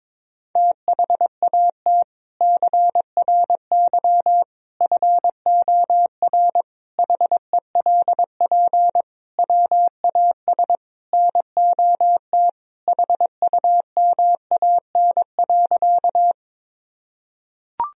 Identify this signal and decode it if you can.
Morse code, heard as a keyed tone, about 22 wpm, 700 Hz. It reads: THAT CRY FOR HELP WAS NOT HUMAN.
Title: Sentences from Top 1000 Words - Review Part 3 - 22wpm